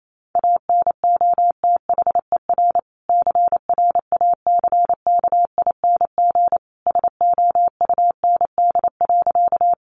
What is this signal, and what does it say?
Morse code, heard as a keyed tone, about 28 wpm, 700 Hz. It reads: ANOTHER CRACKING SOUND.
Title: A Sound Of Thunder 28wpm